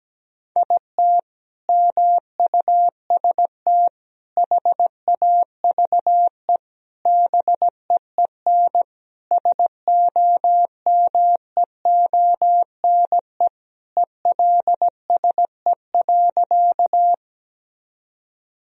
Morse code, heard as a keyed tone, about 17 wpm, 700 Hz. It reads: IT MUST HAVE BEEN SOMEONE ELSE.
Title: Sentences from Top 1000 Words - Review Part 4 - 17wpm